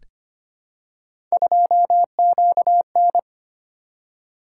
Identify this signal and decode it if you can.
Morse code, heard as a keyed tone, about 25 wpm, 700 Hz. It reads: 2QN
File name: Three Letter-Number Combinations - 25wpm